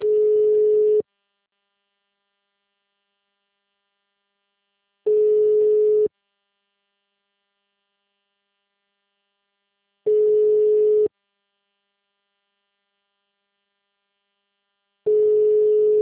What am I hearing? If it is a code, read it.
Ringback tone (Europe, 425 Hz).